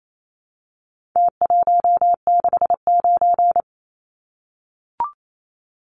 Morse code, heard as a keyed tone, about 28 wpm, 700 Hz. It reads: T169
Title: Numbers - Contest Style - 4-Digit 28wpm